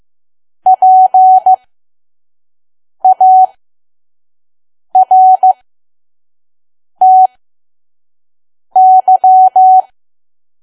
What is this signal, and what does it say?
Morse code, heard as a keyed tone, about 15 wpm, spaced as for 5 wpm, 750 Hz. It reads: PARTY
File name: Archive / 5 WPM files / PM